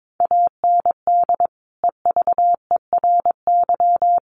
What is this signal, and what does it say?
Morse code, heard as a keyed tone, about 22 wpm, 700 Hz. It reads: AND EVERY
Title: Sentences from Top 1500 Words 22wpm